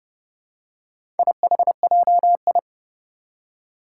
Morse code, heard as a keyed tone, about 30 wpm, 700 Hz. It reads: IHJI
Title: Four Challenging Letter-Number Combinations 30wpm